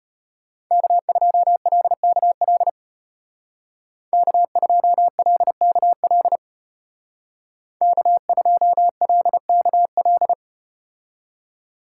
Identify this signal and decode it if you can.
Morse code, heard as a keyed tone, about 34 wpm, 700 Hz. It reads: K2LKL K2LKL K2LKL